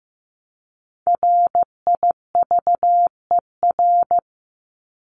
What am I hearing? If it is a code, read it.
Morse code, heard as a keyed tone, about 15 wpm, 700 Hz. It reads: RIVER
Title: Top 500 Words 15wpm